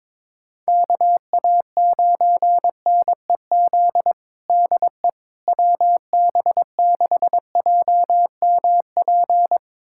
Morse code, heard as a keyed tone, about 22 wpm, 700 Hz. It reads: KA9NEZ DE WB6JMP
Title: USA Callsign DE Callsign 22wpm